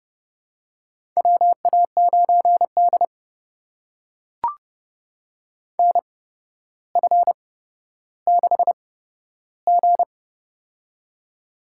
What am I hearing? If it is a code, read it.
Morse code, heard as a keyed tone, about 31 wpm, spaced as for 13 wpm, 700 Hz. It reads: WA9D NF6G
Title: USA Call Signs - 4 Characters - 8x Character Spacing - 30wpm